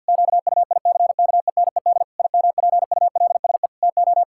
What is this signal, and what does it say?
Morse code, heard as a keyed tone, about 50 wpm, 700 Hz. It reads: QUICKENED IMPULSE TO
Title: Pangram Challenge v2 - 50wpm